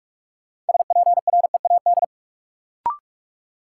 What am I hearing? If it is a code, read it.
Morse code, heard as a keyed tone, about 46 wpm, 700 Hz. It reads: SPREAD
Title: Sentences from Top 1200 Words - New Word Drill Alt - 45wpm